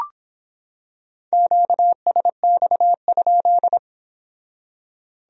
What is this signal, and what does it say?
Morse code, heard as a keyed tone, about 26 wpm, 700 Hz. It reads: QSX?